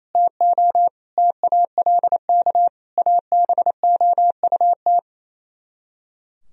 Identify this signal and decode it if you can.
Morse code, heard as a keyed tone, about 34 wpm, 700 Hz. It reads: TO TALK ABOUT
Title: Speed-Racing - Sets of 4 Words 28wpm